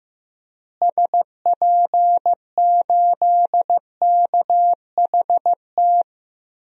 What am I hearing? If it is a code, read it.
Morse code, heard as a keyed tone, about 15 wpm, 700 Hz. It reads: SP8KHT
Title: DXCC-100 Call Signs - 6 Characters - Repeated 2x - 15wpm